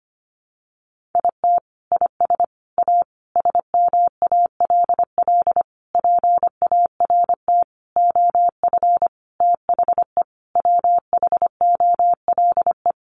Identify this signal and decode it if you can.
Morse code, heard as a keyed tone, about 25 wpm, 700 Hz. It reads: IT IS A SMALL PART OF THE WHOLE
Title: Sentences from Top 1500 Words 25wpm